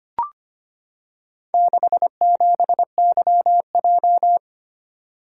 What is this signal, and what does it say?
Morse code, heard as a keyed tone, about 25 wpm, 700 Hz. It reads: 67YJ
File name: practice / 25wpm / Four Letter-Number Combinations - 25wpm